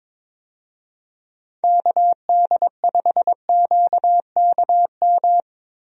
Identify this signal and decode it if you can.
Morse code, heard as a keyed tone, about 22 wpm, 700 Hz. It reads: KD5QKM
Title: USA Call Signs - 6 Characters - Repeated 3x - 22wpm